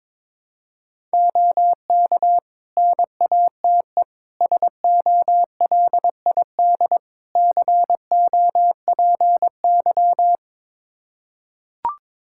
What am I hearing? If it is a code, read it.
Morse code, heard as a keyed tone, about 22 wpm, 700 Hz. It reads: OK NATE SOLID COPY
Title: Practice QSO - Exchange 3 - More Info - 22wpm